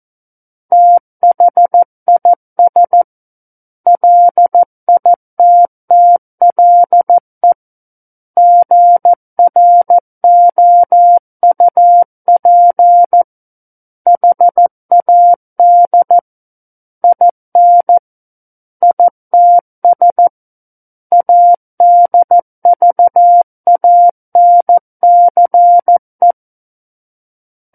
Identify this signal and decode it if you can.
Morse code, heard as a keyed tone, about 14 wpm, 700 Hz. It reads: THIS LITTLE GROUP HAD IN ITS ADVANCE
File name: Book One / 05-Chapter 5 - 14 WPM